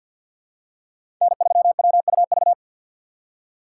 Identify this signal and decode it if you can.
Morse code, heard as a keyed tone, about 51 wpm, 700 Hz. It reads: N3WUV
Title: USA Call Signs - 5 Characters - 4x Character Spacing - 50wpm